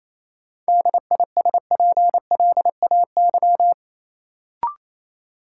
Morse code, heard as a keyed tone, about 28 wpm, 700 Hz. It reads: DISPLAY
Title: Sentences from Top 1200 Words - New Word Drill Alt - 28wpm